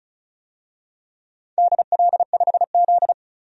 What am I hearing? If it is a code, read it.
Morse code, heard as a keyed tone, about 36 wpm, 700 Hz. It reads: DL5Z
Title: DXCC-100 Call Signs - 4 Characters - 35wpm